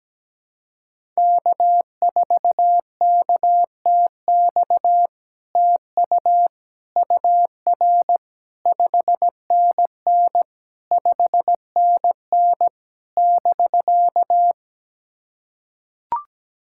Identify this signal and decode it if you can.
Morse code, heard as a keyed tone, about 17 wpm, 700 Hz. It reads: K4KTX TU UR 5NN 5NN <BK>
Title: SOTA QSO Practice 17wpm